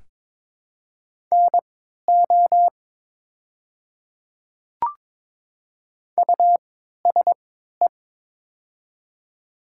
Morse code, heard as a keyed tone, about 22 wpm, spaced as for 12 wpm, 700 Hz. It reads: NO USE
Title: 3x Character Spacing - Top 100 Words - 22wpm